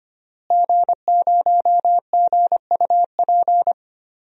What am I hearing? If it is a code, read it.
Morse code, heard as a keyed tone, about 25 wpm, 700 Hz. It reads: G0GUP
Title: DXCC-100 Call Signs - 5 Characters - Repeated 3x - 25wpm